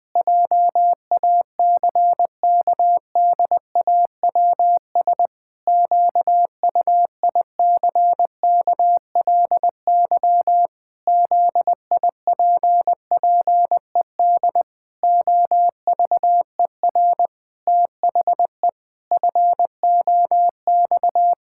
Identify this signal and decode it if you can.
Morse code, heard as a keyed tone, about 20 wpm, 700 Hz. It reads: JACKDAWS QUICKLY ZIPPED OVER THE FOX
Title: Pangram Challenge v2 - 20wpm